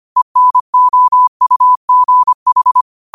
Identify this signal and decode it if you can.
Morse code, heard as a keyed tone, about 25 wpm, 1000 Hz. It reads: ENOUGH